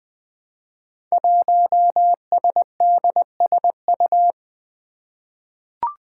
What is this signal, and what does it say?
Morse code, heard as a keyed tone, about 20 wpm, 700 Hz. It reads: 1SDSU
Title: Five Challenging Letter-Number Combinations 20wpm